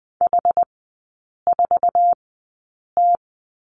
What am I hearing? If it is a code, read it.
Morse code, heard as a keyed tone, about 20 wpm, spaced as for 8 wpm, 700 Hz. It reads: H4T
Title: 107 Lesson 18e - Letter m - Three Characters 20wpm